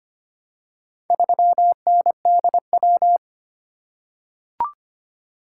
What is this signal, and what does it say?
Morse code, heard as a keyed tone, about 25 wpm, 700 Hz. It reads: 3NDW
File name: practice / 25wpm / Four Letter-Number Combinations - 25wpm